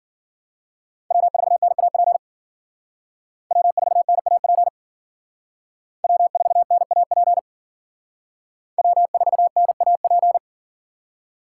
Morse code, heard as a keyed tone, about 52 wpm, 700 Hz. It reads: W4NAP W4NAP W4NAP W4NAP